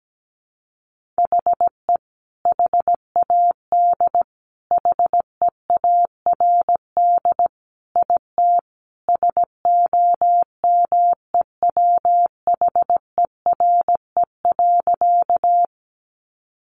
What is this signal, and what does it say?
Morse code, heard as a keyed tone, about 17 wpm, 700 Hz. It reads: HE HAD HEARD IT SOMEWHERE.
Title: Sentences from Top 1400 Words 17wpm